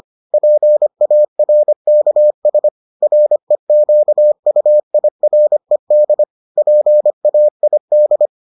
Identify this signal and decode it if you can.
Morse code, heard as a keyed tone, about 25 wpm, 600 Hz. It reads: PARKS REQUIRED PAID